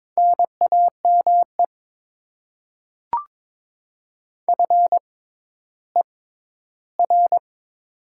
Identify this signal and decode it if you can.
Morse code, heard as a keyed tone, about 22 wpm, spaced as for 12 wpm, 690 Hz. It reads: NAME FER
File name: Top 20 QSO Elements - 6x Character Spacing - 22wpm